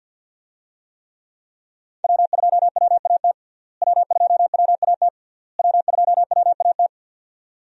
Morse code, heard as a keyed tone, about 49 wpm, 700 Hz. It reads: W2WAT W2WAT W2WAT
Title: USA Call Signs - 5 Characters - Repeated 3x - 50wpm